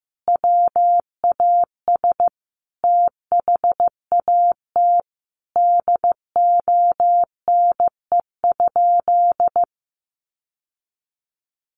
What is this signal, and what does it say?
Morse code, heard as a keyed tone, about 15 wpm, 700 Hz. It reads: WAS THAT DONE?
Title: Sentences from Top 1100 Words 15wpm